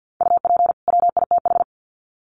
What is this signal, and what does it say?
Morse code, heard as a keyed tone, about 50 wpm, 700 Hz. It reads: UP WITH
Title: Sets of 2 Words 50wpm